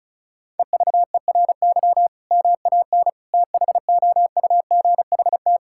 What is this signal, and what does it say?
Morse code, heard as a keyed tone, about 34 wpm, 700 Hz. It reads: EVERY MAN THOUGHT
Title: Sentences from Top 1600 Words 35wpm